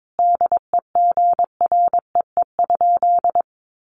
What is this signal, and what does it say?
Morse code, heard as a keyed tone, about 22 wpm, 700 Hz. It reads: DEGREE?
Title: Sentences from Top 1500 Words 22wpm